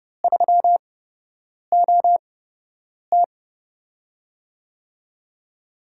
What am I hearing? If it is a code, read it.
Morse code, heard as a keyed tone, about 30 wpm, spaced as for 8 wpm, 700 Hz. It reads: 3OT
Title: USA Call Signs - 4 Characters - 8x Character Spacing - 30wpm